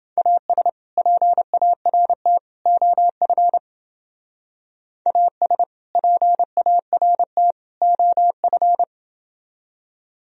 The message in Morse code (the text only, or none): AS PART OF AS PART OF